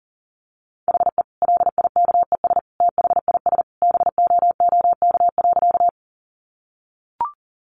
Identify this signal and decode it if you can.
Morse code, heard as a keyed tone, about 40 wpm, 700 Hz. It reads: HE LIKES THIS BOOK.
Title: Additional Easy Phrases 40wpm